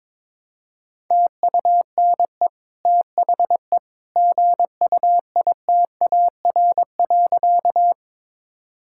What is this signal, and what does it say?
Morse code, heard as a keyed tone, about 21 wpm, 700 Hz. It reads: TUNE THE GUITAR.
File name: practice / 22wpm / Easy Sentences with Hints - 22wpm